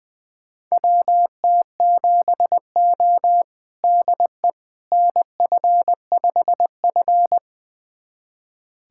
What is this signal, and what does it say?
Morse code, heard as a keyed tone, about 21 wpm, 700 Hz. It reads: WT7O DE NF5F
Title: USA Callsign DE Callsign - 4 Characters - 20wpm